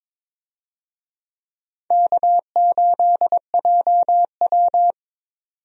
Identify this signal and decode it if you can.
Morse code, heard as a keyed tone, about 21 wpm, 700 Hz. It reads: K8JW
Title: USA Call Signs - 4 Characters - 22wpm